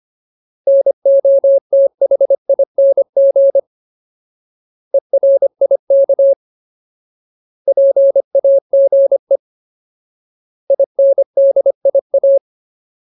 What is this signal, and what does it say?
Morse code, heard as a keyed tone, about 25 wpm, 550 Hz. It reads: NOTHING ERIK PAGE INDIA
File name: Ultimate Word List / 25WPM Ultimate